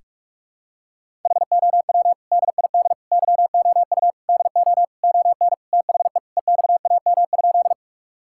Morse code, heard as a keyed tone, about 46 wpm, 700 Hz. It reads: HOW DID YOU DO ON THE EXAM?
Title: Sentences from Top 1000 Words - Review Part 5 - 45wpm